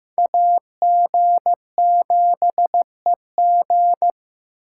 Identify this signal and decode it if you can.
Morse code, heard as a keyed tone, about 18 wpm, 700 Hz. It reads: AG7EG
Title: Speed Racing - USA 2x2 Call Signs 15wpm